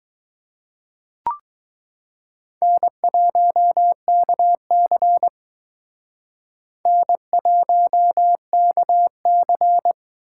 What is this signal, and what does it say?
Morse code, heard as a keyed tone, about 20 wpm, 700 Hz. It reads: N1KC N1KC